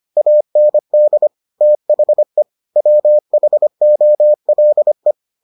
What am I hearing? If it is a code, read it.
Morse code, heard as a keyed tone, about 25 wpm, 600 Hz. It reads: AND THE WHOLE